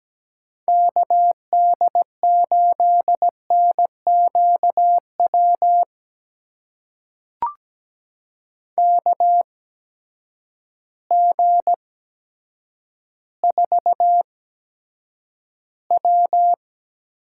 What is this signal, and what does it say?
Morse code, heard as a keyed tone, about 17 wpm, spaced as for 9 wpm, 700 Hz. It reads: KD8NQW KG4W
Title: USA Call Signs - 6 Characters - 8x Character Spacing - 17wpm